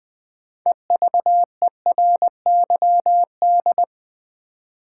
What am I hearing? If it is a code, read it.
Morse code, heard as a keyed tone, about 20 wpm, 700 Hz. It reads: EVERYD
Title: Build A Word - 3 Letters At A Time - 8-Letter Words - 20wpm